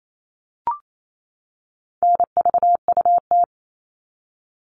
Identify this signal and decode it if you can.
Morse code, heard as a keyed tone, about 28 wpm, 700 Hz. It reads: NVUT
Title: All Cut Numbers - 4-Digit 28wpm